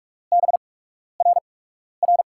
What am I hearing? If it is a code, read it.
Morse code, heard as a keyed tone, about 44 wpm, spaced as for 12 wpm, 690 Hz. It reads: BRR
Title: USA Call Signs - 8x Character Spacing - 45wpm